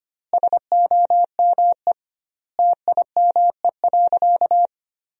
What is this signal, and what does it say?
Morse code, heard as a keyed tone, about 25 wpm, 700 Hz. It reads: SOME TIME.
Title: 2x Word-Spacing - Sentences from Top 300 Words - 25wpm